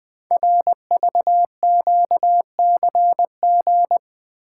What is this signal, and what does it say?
Morse code, heard as a keyed tone, about 20 wpm, 700 Hz. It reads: RVQCG
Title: Code Groups - 5 Letters - 20wpm